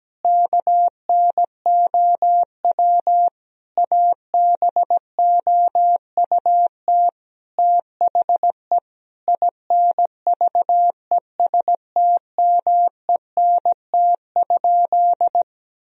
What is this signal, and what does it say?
Morse code, heard as a keyed tone, about 17 wpm, 700 Hz. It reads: KNOW ABOUT THE INVESTMENT?
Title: Sentences from Top 1000 Words - Review Part 4 - 17wpm